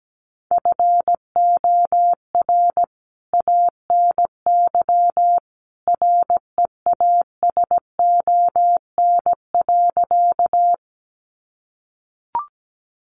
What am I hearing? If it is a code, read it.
Morse code, heard as a keyed tone, about 17 wpm, 700 Hz. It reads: FOR ANY REASON.
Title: A Sound Of Thunder 17wpm